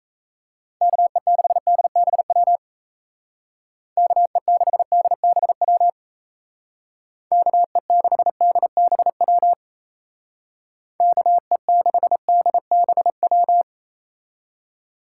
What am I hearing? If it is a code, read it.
Morse code, heard as a keyed tone, about 34 wpm, 700 Hz. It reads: KE6DBW KE6DBW KE6DBW KE6DBW
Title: Speed Racing - USA 2x3 Call Signs 28wpm